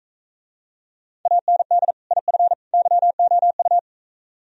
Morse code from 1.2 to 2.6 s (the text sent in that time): AND IF